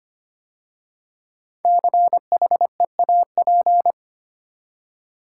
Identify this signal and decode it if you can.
Morse code, heard as a keyed tone, about 25 wpm, 700 Hz. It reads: CHEAP
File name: 5 Letter Words - First 3 Letters Given - 25wpm